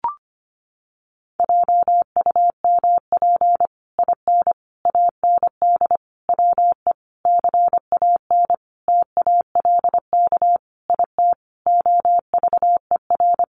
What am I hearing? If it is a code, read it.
Morse code, heard as a keyed tone, about 25 wpm, 700 Hz. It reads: JUMP IN AND WE CAN TALK IT OVER
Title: Sentences from Top 1300 Words 25wpm